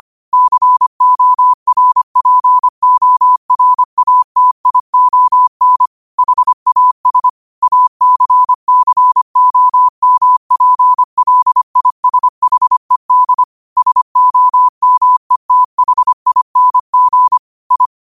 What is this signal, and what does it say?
Morse code, heard as a keyed tone, about 25 wpm, 1000 Hz. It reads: CORPORATION HAS ACCOMPLISHED SOMETHING I
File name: qotdfiles / 25wpm